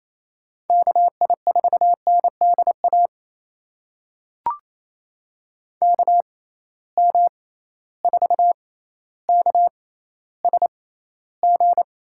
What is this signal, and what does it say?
Morse code, heard as a keyed tone, about 28 wpm, spaced as for 14 wpm, 700 Hz. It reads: KI4NDA KM4KSG
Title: USA Call Signs - 6 Characters - 6x Character Spacing - 28wpm